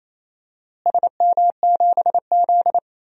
Morse code, heard as a keyed tone, about 28 wpm, 700 Hz. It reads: SM7Z
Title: DXCC-100 Call Signs - 4 Characters - Repeated 3x - 28wpm